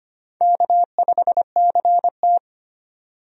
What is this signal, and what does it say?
Morse code, heard as a keyed tone, about 25 wpm, 700 Hz. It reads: K5CT
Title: USA Call Signs - 4 Characters - 25wpm